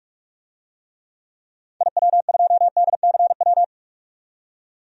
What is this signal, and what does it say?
Morse code, heard as a keyed tone, about 48 wpm, 700 Hz. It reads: IW2DCW
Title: DXCC-100 Call Signs - 6 Characters - 45wpm